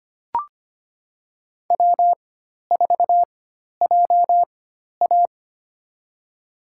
Morse code, heard as a keyed tone, about 25 wpm, spaced as for 12 wpm, 700 Hz. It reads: W4JA
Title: USA Call Signs - 4 Characters - 4x Character Spacing - 25wpm